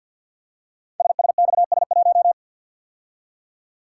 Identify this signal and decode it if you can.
Morse code, heard as a keyed tone, about 50 wpm, 700 Hz. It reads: SSXS1
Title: Code Groups - 5 Letter:Numbers - 50wpm